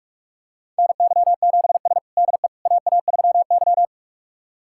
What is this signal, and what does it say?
Morse code, heard as a keyed tone, about 47 wpm, 700 Hz. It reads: NY7S DE AA3Y
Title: USA Callsign DE Callsign - 4 Characters - 45wpm